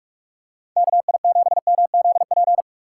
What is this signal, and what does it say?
Morse code, heard as a keyed tone, about 47 wpm, 700 Hz. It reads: KI7MZP
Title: USA Call Signs - 6 Characters - 4x Character Spacing - 45wpm